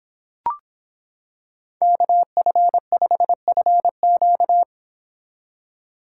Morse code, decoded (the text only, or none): KF5FQ